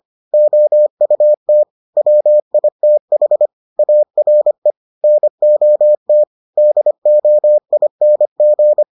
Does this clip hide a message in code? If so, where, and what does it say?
Yes: Morse code from 0.3 to 8.8 s, heard as a keyed tone, about 25 wpm, 600 Hz. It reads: OUT WITH ARE NOT DOING